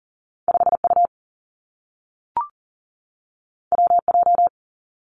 Morse code, heard as a keyed tone, about 40 wpm, 700 Hz. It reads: 5 U W J
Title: Two Challenging Letter-Number Combinations 40wpm